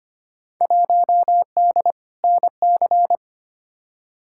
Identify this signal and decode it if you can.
Morse code, heard as a keyed tone, about 25 wpm, 700 Hz. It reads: 1D NC
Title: Field Day Exchanges Encore - 25wpm